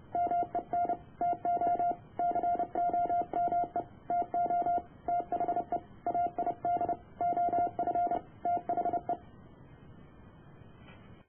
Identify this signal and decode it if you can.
Morse code, heard as a keyed tone, about 30 wpm, 700 Hz. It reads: MEN TO COME TO THE AID OF THE